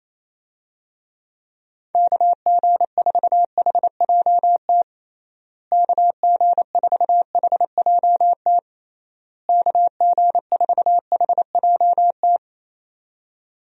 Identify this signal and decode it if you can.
Morse code, heard as a keyed tone, about 28 wpm, 700 Hz. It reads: KG4HJT KG4HJT KG4HJT
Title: USA Call Signs - 6 Characters - Repeated 3x - 28wpm